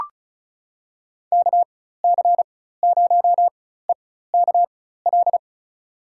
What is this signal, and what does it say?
Morse code, heard as a keyed tone, about 36 wpm, spaced as for 16 wpm, 700 Hz. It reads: KC0EKL
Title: USA Call Signs - 6 Characters - 4x Character Spacing - 35wpm